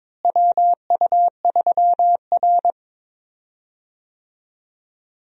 Morse code, heard as a keyed tone, about 22 wpm, 700 Hz. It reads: WU3R
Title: USA Call Signs - 4 Characters - Repeated 2x - 22wpm